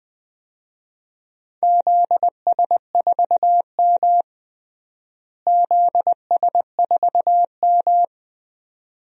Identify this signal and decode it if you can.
Morse code, heard as a keyed tone, about 20 wpm, 700 Hz. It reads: ZS4M ZS4M
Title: DXCC-100 Call Signs - 4 Characters - Repeated 2x - 20wpm